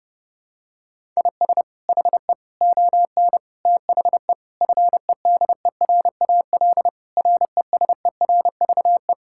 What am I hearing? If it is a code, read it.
Morse code, heard as a keyed tone, about 32 wpm, 700 Hz. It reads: IS HE ON THE FEDERAL RESERVE
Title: Sentences from Top 1000 Words - Review Part 6 - 30wpm